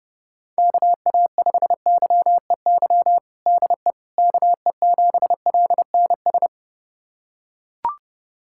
Morse code, heard as a keyed tone, about 29 wpm, 700 Hz. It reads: KA5YEY DE KE7LNS
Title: USA Callsign DE Callsign 30wpm